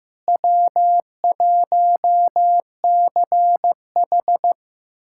Morse code, heard as a keyed tone, about 15 wpm, 700 Hz. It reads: W1CH
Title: USA Callsign DE Callsign - 4 Characters - 15wpm